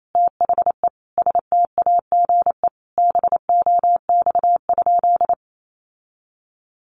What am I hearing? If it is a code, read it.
Morse code, heard as a keyed tone, about 28 wpm, 700 Hz. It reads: THE STAGE BOX?
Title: Sentences from Top 1200 Words 28wpm